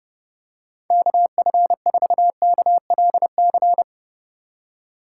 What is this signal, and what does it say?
Morse code, heard as a keyed tone, about 30 wpm, 700 Hz. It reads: KF4KLC
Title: USA Call Signs - 6 Characters - 2x Character Spacing - 30wpm